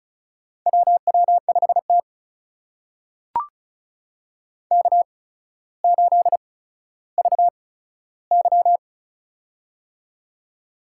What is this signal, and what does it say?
Morse code, heard as a keyed tone, about 35 wpm, spaced as for 15 wpm, 700 Hz. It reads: WW5T K8VY
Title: USA Call Signs - 4 Characters - 8x Character Spacing - 35wpm